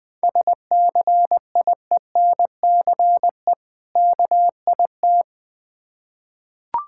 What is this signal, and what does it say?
Morse code, heard as a keyed tone, about 20 wpm, 700 Hz. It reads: SCIENCE KIT
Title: Difficult Words with Hints - 20wpm